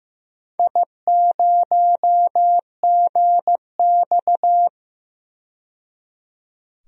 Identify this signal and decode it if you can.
Morse code, heard as a keyed tone, about 15 wpm, 700 Hz. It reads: I0GX
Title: DXCC-100 Call Signs - 4 Characters - 15wpm